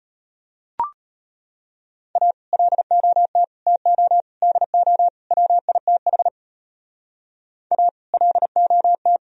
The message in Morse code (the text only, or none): A LOT TO DO WITH A LOT